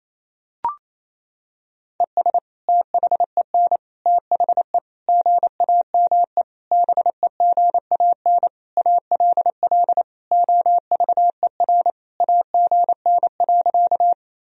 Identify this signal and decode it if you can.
Morse code, heard as a keyed tone, about 28 wpm, 700 Hz. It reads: ES THEN THE GAME BEGAN ALL OVER AGN.
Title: Sentences from Top 500 Words and Abbreviations 28wpm